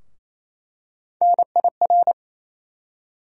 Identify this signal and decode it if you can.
Morse code, heard as a keyed tone, about 28 wpm, 700 Hz. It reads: NIR
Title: USA Call Sign Postfixes - 3 Letters 28wpm